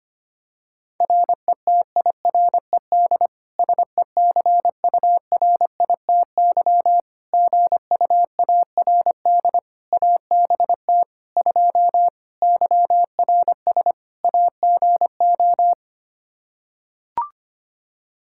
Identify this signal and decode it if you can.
Morse code, heard as a keyed tone, about 25 wpm, 700 Hz. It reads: RETIRED SECURITY GUARD ABT 2 YRS AGO
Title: Practice QSO - Exchange 3 - More Info - 25wpm